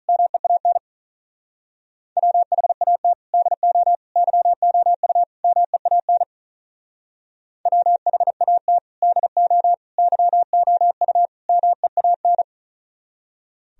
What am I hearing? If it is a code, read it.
Morse code, heard as a keyed tone, about 45 wpm, 700 Hz. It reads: MEAN WHAT DO YOU MEAN WHAT DO YOU MEAN